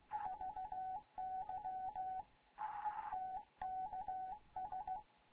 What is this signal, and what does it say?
Morse code, heard as a keyed tone, about 16 wpm, 750 Hz, with radio noise. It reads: VY TKS
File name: Morze-HA2MN / hangp / 2 11 GYAKORLAT